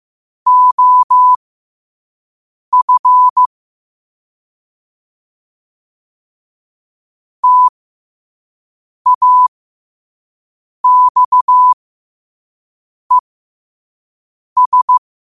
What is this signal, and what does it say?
Morse code, heard as a keyed tone, about 15 wpm, spaced as for 5 wpm, 1000 Hz. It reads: OF TAXES
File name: qotdfiles / 5wpm